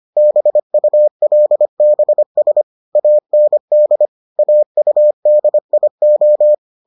Morse code, heard as a keyed tone, about 25 wpm, 600 Hz. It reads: BULBS AND AUDIO